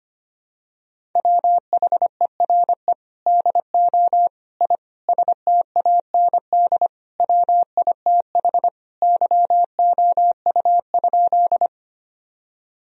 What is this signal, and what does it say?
Morse code, heard as a keyed tone, about 25 wpm, 700 Hz. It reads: WHERE DO I STAND WITH YOU?
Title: Sentences from Top 1000 Words - Review Part 1 - 25wpm